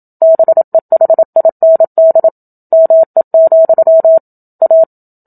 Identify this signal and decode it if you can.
Morse code, heard as a keyed tone, about 27 wpm, 650 Hz. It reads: BEHIND ME, A
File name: Book Two / 01-Chapter 1 - 27 WPM